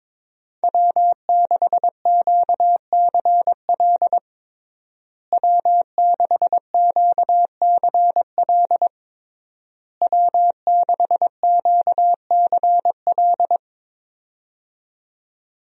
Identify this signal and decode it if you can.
Morse code, heard as a keyed tone, about 21 wpm, 700 Hz. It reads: W6QCL W6QCL W6QCL
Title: USA Call Signs - 5 Characters - Repeated 3x - 22wpm